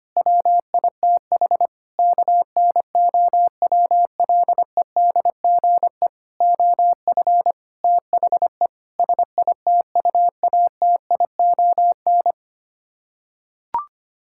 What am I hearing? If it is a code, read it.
Morse code, heard as a keyed tone, about 25 wpm, 700 Hz. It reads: WITH KNOWLEDGE OF THE SITUATION
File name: Sets of 9 Words - 25wpm